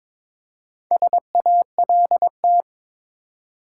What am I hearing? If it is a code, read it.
Morse code, heard as a keyed tone, about 22 wpm, 700 Hz. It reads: SALT
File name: practice / 22wpm / Build A Phrase - 5-Word Phrases - 22wpm